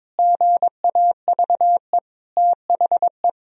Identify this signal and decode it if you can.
Morse code, heard as a keyed tone, about 22 wpm, 700 Hz. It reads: GAVE THE
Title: Sentences from Top 1000 Words - Review Part 4 - 22wpm